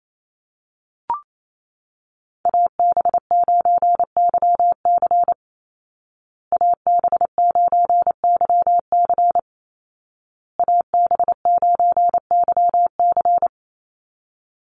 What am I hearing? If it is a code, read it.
Morse code, heard as a keyed tone, about 28 wpm, 700 Hz. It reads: AB9YC AB9YC AB9YC
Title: Call Signs Repeated 2x - 28wpm